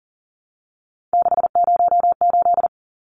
Morse code, heard as a keyed tone, about 40 wpm, 700 Hz. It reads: 608
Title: Numbers - 3-Digit 40wpm